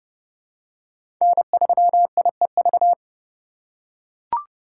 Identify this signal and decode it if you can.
Morse code, heard as a keyed tone, about 30 wpm, 700 Hz. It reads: N3IEV
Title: USA Call Signs - 8x Character Spacing - 30wpm